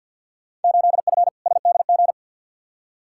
Morse code, heard as a keyed tone, about 48 wpm, 700 Hz. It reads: 8F SDG
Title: Field Day Exchanges Encore - 50wpm